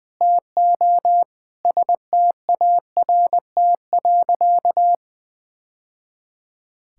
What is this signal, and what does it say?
Morse code, heard as a keyed tone, about 21 wpm, 700 Hz. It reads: TO START.
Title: Sentences from Top 1000 Words - Review Part 5 - 20wpm